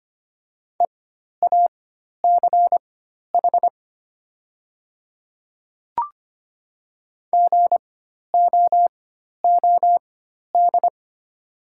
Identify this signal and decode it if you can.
Morse code, heard as a keyed tone, about 25 wpm, spaced as for 12 wpm, 700 Hz. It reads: EACH GOOD
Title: 4x Character Spacing - Top 500 Words - 25wpm